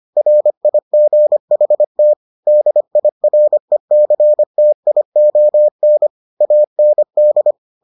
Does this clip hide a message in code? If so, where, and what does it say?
Yes: Morse code from 0.2 to 7.5 s, heard as a keyed tone, about 25 wpm, 600 Hz. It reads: RIGHT DIRECTION AND